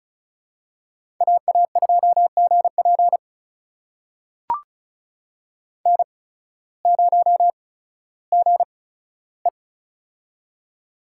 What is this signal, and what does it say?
Morse code, heard as a keyed tone, about 36 wpm, spaced as for 16 wpm, 700 Hz. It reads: AA2GP N0GE
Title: USA Call Signs - 8x Character Spacing - 35wpm